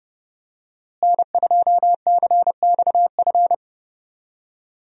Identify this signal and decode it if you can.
Morse code, heard as a keyed tone, about 30 wpm, 700 Hz. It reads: N2CXF
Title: USA Call Signs - 5 Characters - Repeated 3x - 30wpm